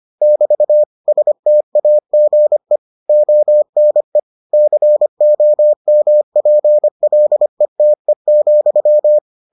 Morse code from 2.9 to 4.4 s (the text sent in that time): ONE